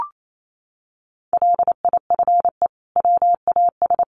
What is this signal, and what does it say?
Morse code, heard as a keyed tone, about 28 wpm, 700 Hz. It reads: LIFE WAS
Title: Sentences from Top 1200 Words 28wpm